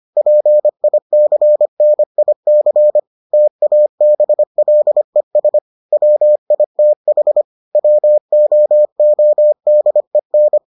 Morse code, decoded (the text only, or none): PICNIC TABLES WITH WOODEN